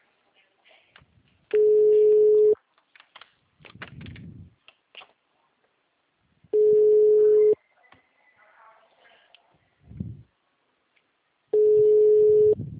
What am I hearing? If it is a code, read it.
Ringback tone (Europe, 425 Hz).
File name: out- -2060- - - .48